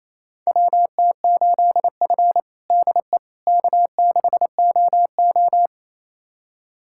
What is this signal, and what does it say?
Morse code, heard as a keyed tone, about 28 wpm, 700 Hz. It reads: WT8F DE K6OO
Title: USA Callsign DE Callsign - 4 Characters - 28wpm